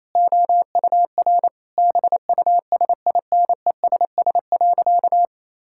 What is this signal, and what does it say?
Morse code, heard as a keyed tone, about 28 wpm, 700 Hz. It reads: OUR BUSINESS.
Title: Sentences from Top 1000 Words - Review Part 6 - 28wpm